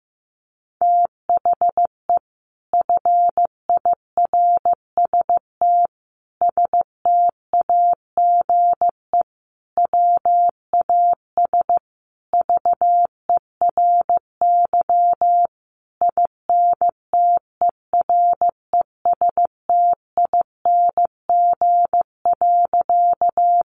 Morse code, heard as a keyed tone, about 15 wpm, 700 Hz. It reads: THE FIRST STAGE WAS VERY INTERESTING.
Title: Sentences from Top 1200 Words 15wpm